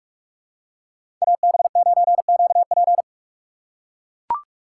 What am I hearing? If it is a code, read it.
Morse code, heard as a keyed tone, about 44 wpm, 700 Hz. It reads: AB9QP
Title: USA Call Signs - 5 Characters - 3x Character Spacing - 45wpm